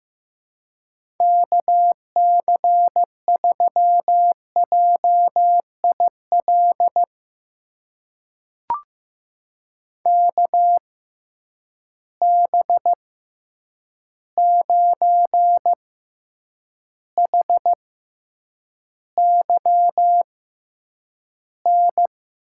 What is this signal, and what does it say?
Morse code, heard as a keyed tone, about 15 wpm, spaced as for 8 wpm, 700 Hz. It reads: KC3JIL KB9HYN
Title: USA Call Signs - 6 Characters - 6x Character Spacing - 15wpm